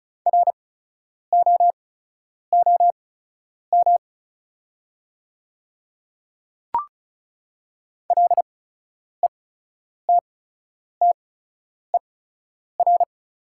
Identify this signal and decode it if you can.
Morse code, heard as a keyed tone, about 35 wpm, spaced as for 10 wpm, 690 Hz. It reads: ROOM LETTER
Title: 8x Character Spacing - Top 300 Words - 35wpm